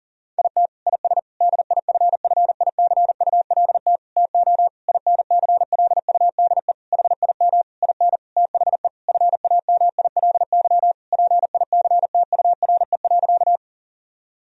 Morse code, heard as a keyed tone, about 40 wpm, 700 Hz. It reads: IT IS DIFFICULT TO INCLUDE HIM IN THE FAMILY PICTURE.
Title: Sentences from Top 1000 Words - Review Part 3 - 40wpm